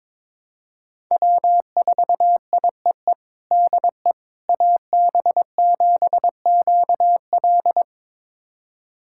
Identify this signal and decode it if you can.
Morse code, heard as a keyed tone, about 22 wpm, 700 Hz. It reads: W4IEE DE AB7QL
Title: USA Callsign DE Callsign - 5 Characters - 22wpm